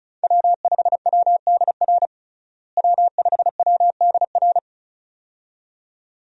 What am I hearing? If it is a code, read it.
Morse code, heard as a keyed tone, about 34 wpm, 700 Hz. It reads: W5WDR W5WDR
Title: USA Call Signs - 5 Characters - Repeated 2x - 35wpm